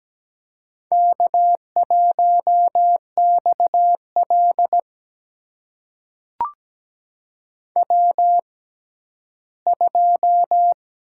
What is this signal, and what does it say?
Morse code, heard as a keyed tone, about 17 wpm, spaced as for 11 wpm, 700 Hz. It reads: K1XL W2
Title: USA Call Signs - 4 Characters - 6x Character Spacing - 17wpm